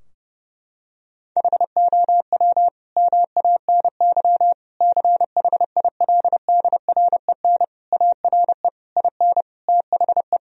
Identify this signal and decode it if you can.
Morse code, heard as a keyed tone, about 32 wpm, 700 Hz. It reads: HOW MANY CHILDREN ARE IN THE
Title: Sentences from Top 1000 Words - Review Part 4 - 30wpm